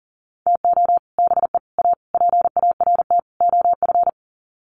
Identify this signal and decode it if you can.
Morse code, heard as a keyed tone, about 37 wpm, 700 Hz. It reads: TO BE A PART OF
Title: Sets of 5 Words 40wpm